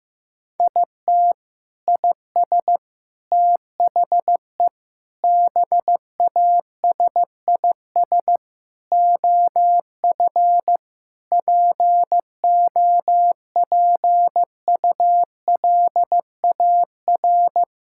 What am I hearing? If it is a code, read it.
Morse code, heard as a keyed tone, about 15 wpm, 700 Hz. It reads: IT IS THE BASIS OF POPULAR
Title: Sentences from Top 1000 Words - Review Part 5 - 15wpm